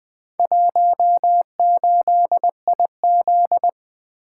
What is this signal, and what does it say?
Morse code, heard as a keyed tone, about 20 wpm, 700 Hz. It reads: 18IZ
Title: Four Letter-Number Combinations - 20wpm